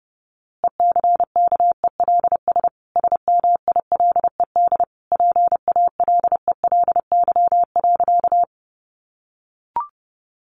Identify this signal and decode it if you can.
Morse code, heard as a keyed tone, about 30 wpm, 700 Hz. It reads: ECKELS SMILED PALELY.
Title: A Sound Of Thunder 30wpm